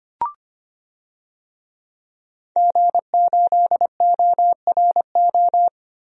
Morse code, heard as a keyed tone, about 25 wpm, 700 Hz. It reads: G8ORO
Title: DXCC-100 Call Signs - 5 Characters - Repeated 3x - 25wpm